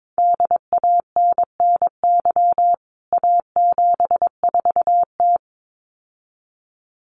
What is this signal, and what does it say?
Morse code, heard as a keyed tone, about 22 wpm, 700 Hz. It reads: DANNY A74T
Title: CWT Exchange 22wpm